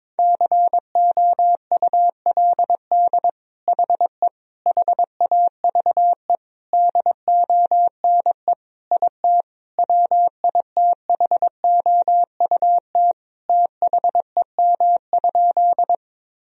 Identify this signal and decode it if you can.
Morse code, heard as a keyed tone, about 22 wpm, 700 Hz. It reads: COULD HE HAVE DONE IT WITHOUT THEM?
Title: Sentences from Top 1000 Words - Review Part 3 - 22wpm